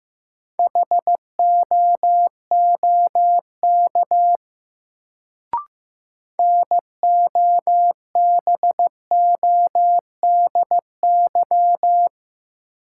Morse code, heard as a keyed tone, about 15 wpm, 700 Hz. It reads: HOOK NOBODY
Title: Sentences from Top 1600 Words - New Word Drill Alt - 15wpm